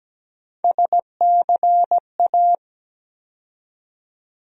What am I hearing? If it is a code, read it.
Morse code, heard as a keyed tone, about 17 wpm, 700 Hz. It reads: SCA
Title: USA Call Sign Postfixes - 3 Letters 17wpm